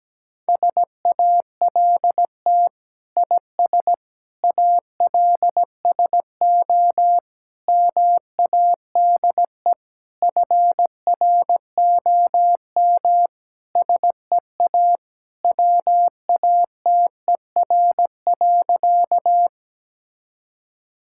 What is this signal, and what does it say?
Morse code, heard as a keyed tone, about 17 wpm, 700 Hz. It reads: SALT IS ALSO MADE FROM SEA WATER.
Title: Sentences from Top 1000 Words - Review Part 5 - 17wpm